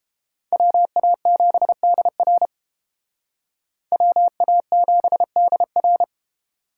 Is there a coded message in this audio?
Yes: Morse code from 0.5 to 6.0 s, heard as a keyed tone, about 30 wpm, 700 Hz. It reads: WA7DR WA7DR